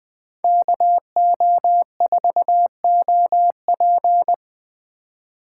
Morse code, heard as a keyed tone, about 20 wpm, 700 Hz. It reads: KO4OP